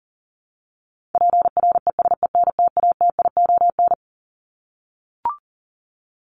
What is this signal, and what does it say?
Morse code, heard as a keyed tone, about 40 wpm, 700 Hz. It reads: PRESENTATION
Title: Top 1501-1750 Words 40wpm